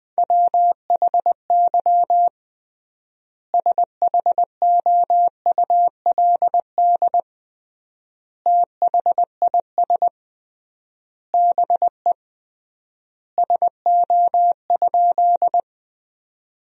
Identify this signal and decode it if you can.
Morse code, heard as a keyed tone, about 20 wpm, 700 Hz. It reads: WHY SHOULD THIS BE SO?
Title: 3x Word-Spacing - Sentences from Top 300 Words - 20wpm